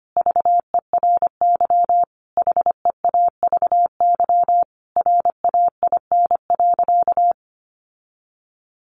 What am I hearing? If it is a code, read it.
Morse code, heard as a keyed tone, about 25 wpm, 700 Hz. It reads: VERY HEAVY RAIN.
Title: Sentences from Top 1100 Words 25wpm